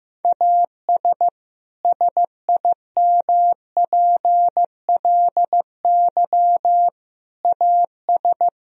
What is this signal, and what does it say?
Morse code, heard as a keyed tone, about 15 wpm, 700 Hz. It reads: AS SIMPLY AS
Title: Sentences from Top 1000 Words - Review Part 4 - 15wpm